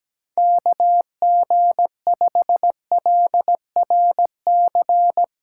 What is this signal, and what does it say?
Morse code, heard as a keyed tone, about 17 wpm, 700 Hz. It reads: KG5LRC
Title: USA Call Signs - 6 Characters - 6x Character Spacing - 17wpm